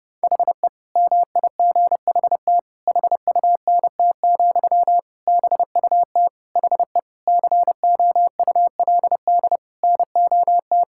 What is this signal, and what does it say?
Morse code, heard as a keyed tone, about 32 wpm, 700 Hz. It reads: HE MIGHT HUNT, BUT HE COULD NOT
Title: Sentences from Top 1600 Words 30wpm